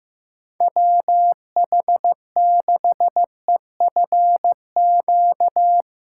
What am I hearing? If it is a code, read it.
Morse code, heard as a keyed tone, about 15 wpm, 700 Hz. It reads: WH6EFQ
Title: USA Call Signs - 6 Characters - 3x Character Spacing - 15wpm